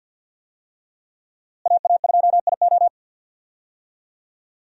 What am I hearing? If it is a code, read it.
Morse code, heard as a keyed tone, about 51 wpm, 700 Hz. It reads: AA2IO
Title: USA Call Signs - 5 Characters - Repeated 3x - 50wpm